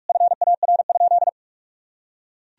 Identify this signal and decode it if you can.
Morse code, heard as a keyed tone, about 43 wpm, 700 Hz. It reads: FAR?
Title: Sentences from Top 300 Words and Abbreviations 45wpm